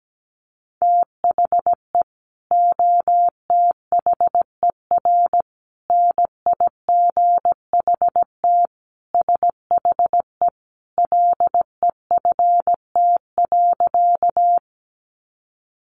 Morse code, heard as a keyed tone, about 17 wpm, 700 Hz. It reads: THE OTHER NIGHT SHE LEFT.
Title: Sentences from Top 300 Words 17wpm